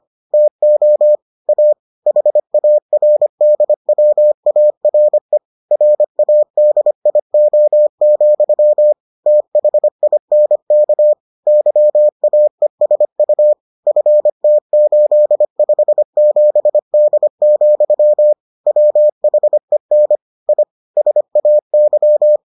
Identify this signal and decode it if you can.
Morse code, heard as a keyed tone, about 25 wpm, 600 Hz. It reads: TO A HARDWARE RADIO, THINK YAESU FT857D, WHEN I SAY